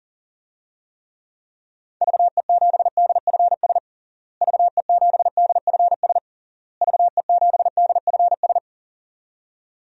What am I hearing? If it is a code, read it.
Morse code, heard as a keyed tone, about 40 wpm, 700 Hz. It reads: VE7DFS VE7DFS VE7DFS
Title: DXCC-100 Call Signs - 6 Characters - Repeated 3x - 40wpm